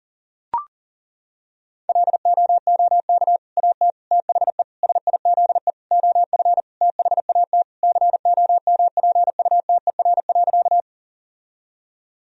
Morse code, heard as a keyed tone, about 37 wpm, 700 Hz. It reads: LOOK AT THE SIZE OF THAT COMPUTER.
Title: Sentences from Top 1000 Words - Review Part 3 - 40wpm